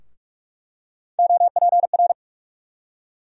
Morse code, heard as a keyed tone, about 45 wpm, 700 Hz. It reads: OPR
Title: USA Call Sign Postfixes - 3 Letters 45wpm